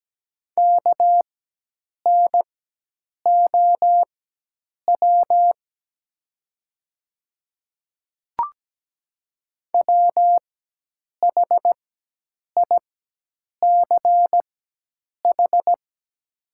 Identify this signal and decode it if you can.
Morse code, heard as a keyed tone, about 17 wpm, spaced as for 8 wpm, 700 Hz. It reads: KNOW WHICH
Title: 4x Character Spacing - Top 200 Words - 17wpm